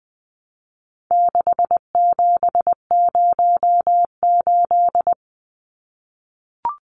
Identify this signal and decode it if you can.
Morse code, heard as a keyed tone, about 20 wpm, 700 Hz. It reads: 6708
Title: Numbers - 4-Digit 20wpm